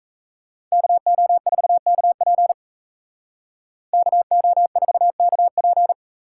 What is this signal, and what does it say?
Morse code, heard as a keyed tone, about 36 wpm, 700 Hz. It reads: KO4KP KO4KP